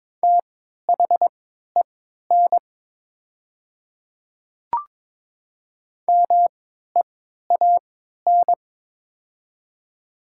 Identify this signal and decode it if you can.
Morse code, heard as a keyed tone, about 22 wpm, spaced as for 12 wpm, 700 Hz. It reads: THEN MEAN
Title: 3x Character Spacing - Top 300 Words - 22wpm